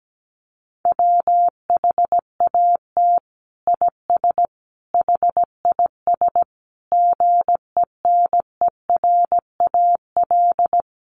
Morse code, heard as a keyed tone, about 17 wpm, 700 Hz. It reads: WHAT IS HIS GENERAL
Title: Sentences from Top 1000 Words 17wpm